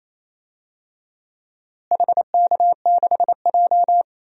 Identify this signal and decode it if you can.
Morse code, heard as a keyed tone, about 28 wpm, 700 Hz. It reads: HK6J
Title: DXCC-100 Call Signs - 4 Characters - Repeated 3x - 28wpm